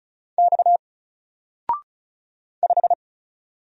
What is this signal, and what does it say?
Morse code, heard as a keyed tone, about 35 wpm, 700 Hz. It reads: X 5
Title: Single Letter-Number Alt 35wpm